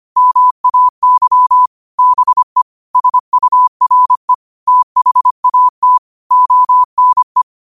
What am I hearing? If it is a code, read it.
Morse code, heard as a keyed tone, about 25 wpm, 1000 Hz. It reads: MAY BE SURE THAT ONE